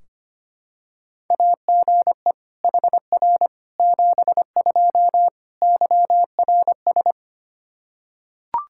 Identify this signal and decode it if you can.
Morse code, heard as a keyed tone, about 25 wpm, 700 Hz. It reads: AGE HR 72 YRS
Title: K1USN SST QSO Practice 25wpm